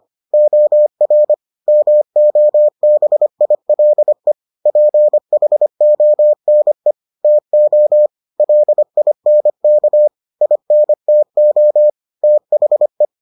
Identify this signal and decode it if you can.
Morse code, heard as a keyed tone, about 25 wpm, 600 Hz. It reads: OR MOBILE PHONE TO LINK INTO THE